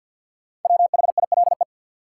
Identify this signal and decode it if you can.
Morse code, heard as a keyed tone, about 53 wpm, 700 Hz. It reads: WHILE
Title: Sentences from Top 300 Words - New Word Drill - 50wpm